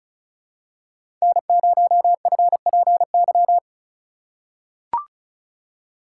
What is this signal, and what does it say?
Morse code, heard as a keyed tone, about 36 wpm, 700 Hz. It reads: N0FPY
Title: USA Call Signs - 5 Characters - 4x Character Spacing - 35wpm